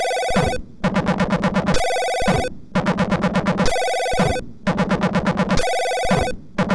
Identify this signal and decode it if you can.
There is no code.